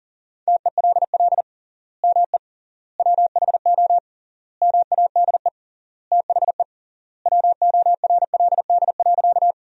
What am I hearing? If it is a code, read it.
Morse code, heard as a keyed tone, about 40 wpm, 700 Hz. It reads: TELL ME WHO MADE THE WORLD.
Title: 3x Word-Spacing - Sentences from Top 200 Words - 40wpm